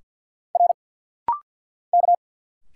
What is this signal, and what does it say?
Morse code, heard as a keyed tone, about 50 wpm, 700 Hz. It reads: R K
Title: Single Letters - Rapid-Fire 50wpm